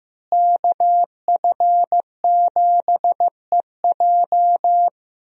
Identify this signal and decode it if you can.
Morse code, heard as a keyed tone, about 15 wpm, 700 Hz. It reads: KF7EJ
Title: USA Call Signs - 5 Characters - Repeated 3x - 15wpm